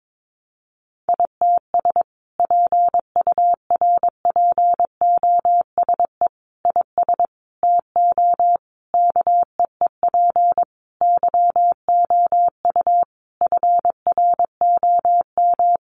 Morse code, heard as a keyed tone, about 22 wpm, 700 Hz. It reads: ITS PURPOSE IS TO KEEP YOU FROM
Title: A Sound Of Thunder 22wpm